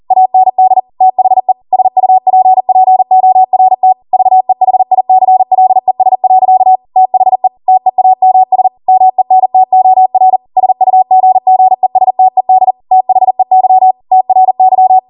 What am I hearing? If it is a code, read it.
Morse code, heard as a keyed tone, about 40 wpm, 750 Hz. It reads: AND THE SUPPORT VEHICLES. THE TEAMS MENTOR SUGGESTED THEY TRY